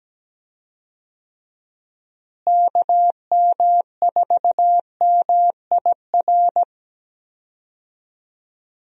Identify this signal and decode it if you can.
Morse code, heard as a keyed tone, about 17 wpm, 700 Hz. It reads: KM4MIR
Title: USA Call Signs - 6 Characters - Repeated 2x - 17wpm